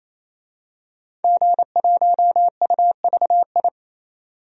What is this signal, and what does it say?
Morse code, heard as a keyed tone, about 28 wpm, 700 Hz. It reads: G1UVI
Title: DXCC-100 Call Signs - 5 Characters - 28wpm